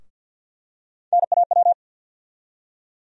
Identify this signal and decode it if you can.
Morse code, heard as a keyed tone, about 53 wpm, 700 Hz. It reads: NAW
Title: USA Call Sign Postfixes - 3 Letters 50wpm